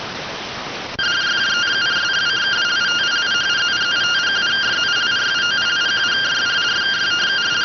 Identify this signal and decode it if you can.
Two-tone teleprinter signal, about 170 Hz shift.